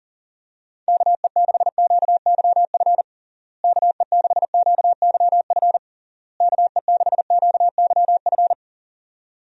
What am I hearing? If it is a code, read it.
Morse code, heard as a keyed tone, about 40 wpm, 700 Hz. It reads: KE6QYF KE6QYF KE6QYF
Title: USA Call Signs - 6 Characters - Repeated 3x - 40wpm